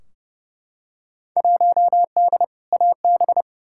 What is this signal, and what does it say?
Morse code, heard as a keyed tone, about 30 wpm, 700 Hz. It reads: 1D AB
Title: Field Day Exchanges Encore - 30wpm